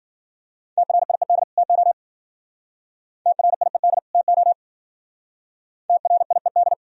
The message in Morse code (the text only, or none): TRIED TO TRIED TO TRIED